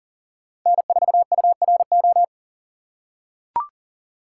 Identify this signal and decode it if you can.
Morse code, heard as a keyed tone, about 40 wpm, 700 Hz. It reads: N4URO
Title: USA Call Signs - 5 Characters - 6x Character Spacing - 40wpm